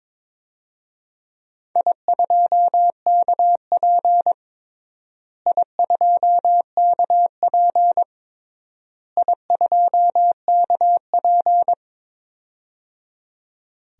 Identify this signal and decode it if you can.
Morse code, heard as a keyed tone, about 22 wpm, 700 Hz. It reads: I2KP I2KP I2KP
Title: DXCC-100 Call Signs - 4 Characters - Repeated 3x - 22wpm